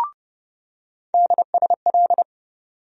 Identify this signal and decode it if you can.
Morse code, heard as a keyed tone, about 30 wpm, 700 Hz. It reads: DSL
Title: USA Call Sign Postfixes - 3 Letters 30wpm